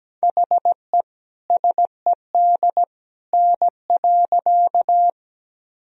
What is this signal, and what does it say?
Morse code, heard as a keyed tone, about 17 wpm, 700 Hz. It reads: HE SED N.
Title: Sentences from Top 100 Words and Abbreviations 17wpm